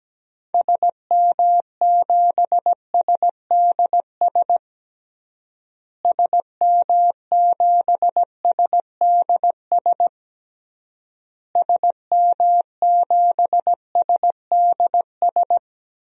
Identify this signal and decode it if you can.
Morse code, heard as a keyed tone, about 17 wpm, 700 Hz. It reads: SM7SDS SM7SDS SM7SDS
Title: DXCC-100 Call Signs - 6 Characters - Repeated 3x - 17wpm